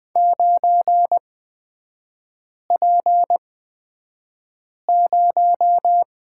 Morse code, heard as a keyed tone, about 20 wpm, spaced as for 5 wpm, 700 Hz. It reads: 9P0
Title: 243 Lesson 39e - Letter z - Three Characters 20wpm